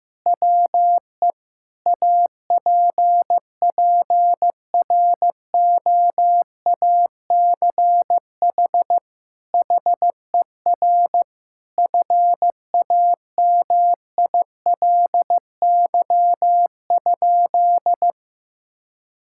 Morse code, heard as a keyed tone, about 15 wpm, 700 Hz. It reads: WE APPROACH HER FAMILY?
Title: Sentences from Top 1000 Words - Review Part 6 - 15wpm